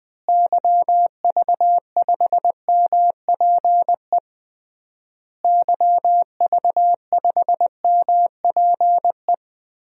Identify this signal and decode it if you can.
Morse code, heard as a keyed tone, about 19 wpm, 700 Hz. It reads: YV5MPE YV5MPE
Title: DXCC-100 Call Signs - 6 Characters - Repeated 3x - 20wpm